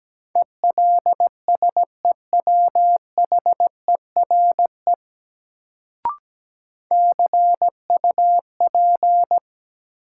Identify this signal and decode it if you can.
Morse code, heard as a keyed tone, about 17 wpm, 700 Hz. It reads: ELSEWHERE CUP
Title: Sentences from Top 1600 Words - New Word Drill - 17wpm